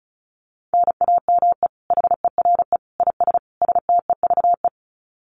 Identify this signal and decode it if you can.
Morse code, heard as a keyed tone, about 34 wpm, 700 Hz. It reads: NAME HERE IS STEVE
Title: Common QSO Phrases 35wpm